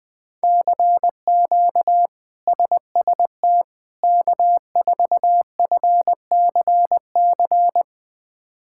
Morse code, heard as a keyed tone, about 20 wpm, 700 Hz. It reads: CQ SST K4FCC
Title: K1USN SST QSO Practice - 20wpm